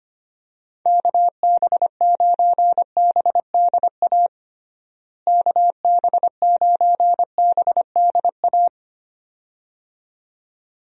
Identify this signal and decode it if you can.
Morse code, heard as a keyed tone, about 25 wpm, 700 Hz. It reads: KB9BDA KB9BDA
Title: USA Call Signs - 6 Characters - Repeated 2x - 25wpm